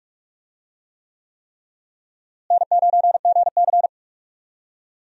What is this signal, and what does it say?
Morse code, heard as a keyed tone, about 46 wpm, 700 Hz. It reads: N9GC
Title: USA Call Signs - 4 Characters - Repeated 3x - 45wpm